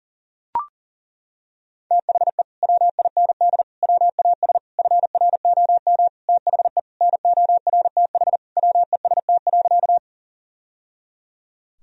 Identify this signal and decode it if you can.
Morse code, heard as a keyed tone, about 37 wpm, 700 Hz. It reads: THE WIND WAS FROM THE NORTH WEST.
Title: Sentences from Top 1000 Words - Review Part 2 - 40wpm